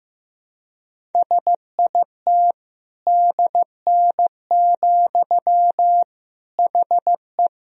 Morse code, heard as a keyed tone, about 15 wpm, 700 Hz. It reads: SIT DN, HE
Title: Sentences from Top 500 Words and Abbreviations 15wpm